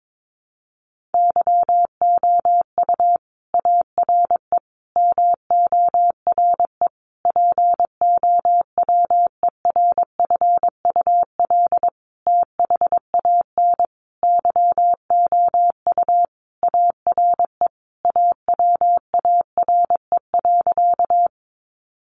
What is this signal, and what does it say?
Morse code, heard as a keyed tone, about 22 wpm, 700 Hz. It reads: YOU ARE MORE POWERFUL THAN YOU ARE AWARE.
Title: Sentences from Top 1300 Words 22wpm